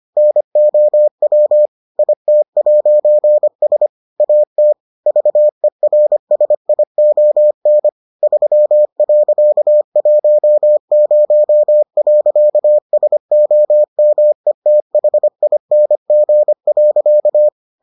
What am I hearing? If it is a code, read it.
Morse code, heard as a keyed tone, about 25 wpm, 600 Hz. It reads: NOW IT'S AT VERSION 3.10.SOMETHING.